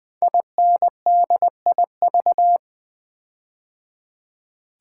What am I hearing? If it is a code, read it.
Morse code, heard as a keyed tone, about 20 wpm, 700 Hz. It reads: INDIV
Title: Build A Word - 5 Letters At A Time - 10-Letter Words - 20wpm